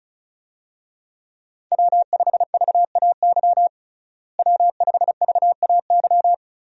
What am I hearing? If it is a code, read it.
Morse code, heard as a keyed tone, about 36 wpm, 700 Hz. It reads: W5VAY W5VAY
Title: USA Call Signs - 5 Characters - Repeated 2x - 35wpm